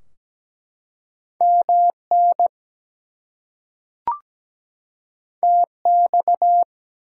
Morse code, heard as a keyed tone, about 17 wpm, 700 Hz. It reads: MN TX
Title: US States and Canadian Provinces 17wpm